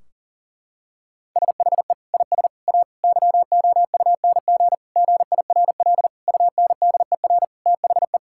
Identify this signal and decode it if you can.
Morse code, heard as a keyed tone, about 40 wpm, 700 Hz. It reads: SHE IS A YOUNG GIRL UNDER THE
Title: Sentences from Top 1600 Words 40wpm